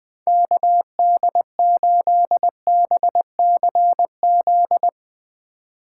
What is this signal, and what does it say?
Morse code, heard as a keyed tone, about 20 wpm, 700 Hz. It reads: KD8BCZ
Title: Speed Racing - USA Call Signs 17wpm